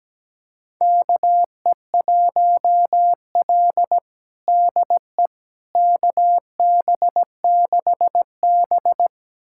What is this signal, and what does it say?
Morse code, heard as a keyed tone, about 17 wpm, 700 Hz. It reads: KE1L DE KB6B
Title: USA Callsign DE Callsign - 4 Characters - 17wpm